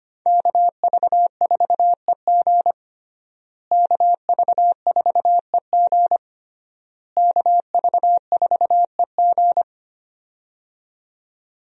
Morse code, heard as a keyed tone, about 25 wpm, 700 Hz. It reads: KV4EG KV4EG KV4EG
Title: USA Call Signs - 5 Characters - Repeated 3x - 25wpm